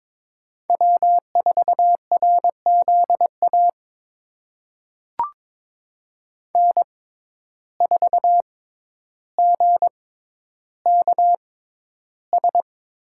Morse code, heard as a keyed tone, about 22 wpm, spaced as for 11 wpm, 700 Hz. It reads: W4RZA N4GKS
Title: USA Call Signs - 5 Characters - 6x Character Spacing - 22wpm